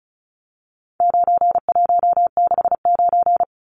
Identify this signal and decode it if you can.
Morse code, heard as a keyed tone, about 34 wpm, 700 Hz. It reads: 9169
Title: Numbers - 4-Digit 35wpm